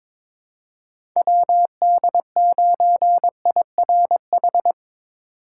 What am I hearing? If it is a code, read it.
Morse code, heard as a keyed tone, about 21 wpm, 700 Hz. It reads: WD9IRH
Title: USA Call Signs - 6 Characters - 3x Character Spacing - 22wpm